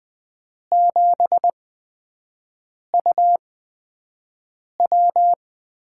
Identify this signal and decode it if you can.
Morse code, heard as a keyed tone, about 20 wpm, 700 Hz. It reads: 7 U W
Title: USA Call Signs - 5 Characters - 8x Character Spacing - 20wpm